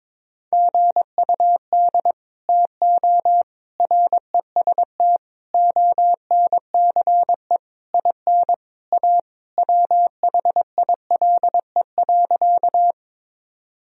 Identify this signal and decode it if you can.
Morse code, heard as a keyed tone, about 22 wpm, 700 Hz. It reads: GUD TO REST ONCE IN A WHILE.
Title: Sentences from Top 500 Words and Abbreviations 22wpm